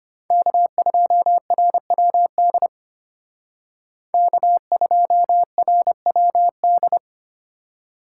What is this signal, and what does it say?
Morse code, heard as a keyed tone, about 32 wpm, 700 Hz. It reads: K2RWD K2RWD